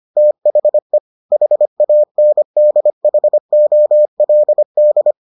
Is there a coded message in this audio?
Yes: Morse code from 0.2 to 5.1 s, heard as a keyed tone, about 25 wpm, 600 Hz. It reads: THE HANDHOLD